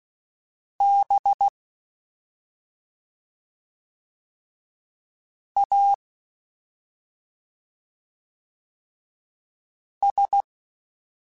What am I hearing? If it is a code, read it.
Morse code, heard as a keyed tone, about 16 wpm, 800 Hz. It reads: B A S
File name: Special INGs-mp3s / Special INGs-mp3s / Slow ing @10